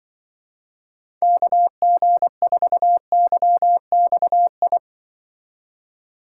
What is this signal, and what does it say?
Morse code, heard as a keyed tone, about 24 wpm, 700 Hz. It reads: KG4YXI
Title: Speed Racing - USA 2x3 Call Signs 20wpm